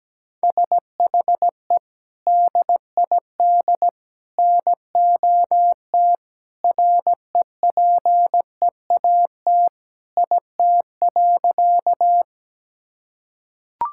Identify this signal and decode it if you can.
Morse code, heard as a keyed tone, about 17 wpm, 700 Hz. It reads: SHE DID NOT REPEAT IT.
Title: Sentences from Top 1600 Words 17wpm